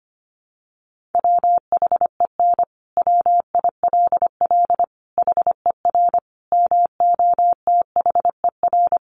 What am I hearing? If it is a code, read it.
Morse code, heard as a keyed tone, about 25 wpm, 700 Hz. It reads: WHEN WILL HER MOTHER
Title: Sentences from Top 200 Words 25wpm